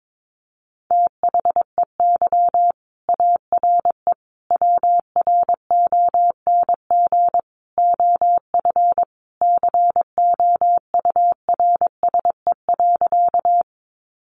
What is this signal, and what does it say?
Morse code, heard as a keyed tone, about 22 wpm, 700 Hz. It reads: THEY ARE WRONG OF COURSE.
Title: Sentences from Top 900 Words 22wpm